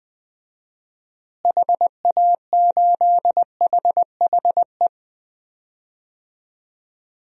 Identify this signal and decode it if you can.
Morse code, heard as a keyed tone, about 21 wpm, 700 Hz. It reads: HA8HHE
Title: DXCC-100 Call Signs - 6 Characters - Repeated 2x - 20wpm